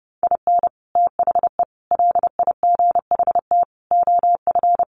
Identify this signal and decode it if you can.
Morse code, heard as a keyed tone, about 30 wpm, 700 Hz. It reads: IN THE LIGHT OF
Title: Sets of 5 Words Encore2 30wpm